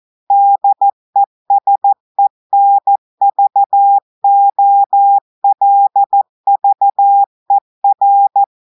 Morse code, heard as a keyed tone, about 14 wpm, 800 Hz. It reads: DESENVOLVER
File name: audios / portugais / portugais texte en clair 2 vitesse 14 mots minutes